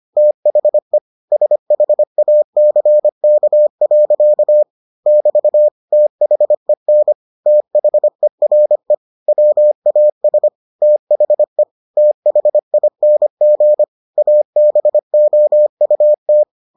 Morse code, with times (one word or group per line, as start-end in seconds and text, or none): 0.2-1.0 THE
1.3-4.6 SHACK.
5.1-5.7 <BT>
5.9-7.1 THEN
7.5-8.9 THERE
9.3-10.5 WAS
10.8-11.6 THE
12.0-13.8 THING
14.2-16.4 ABOUT